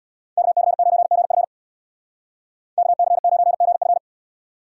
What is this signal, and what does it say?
Morse code, heard as a keyed tone, about 64 wpm, 700 Hz. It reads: KC9GF KC9GF